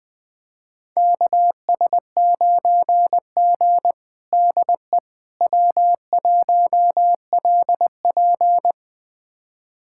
Morse code, heard as a keyed tone, about 21 wpm, 700 Hz. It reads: KS9G DE W1LP
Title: USA Callsign DE Callsign - 4 Characters - 20wpm